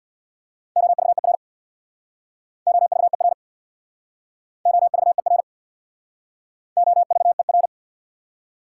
Morse code, heard as a keyed tone, about 65 wpm, 700 Hz. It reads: OVER OVER OVER OVER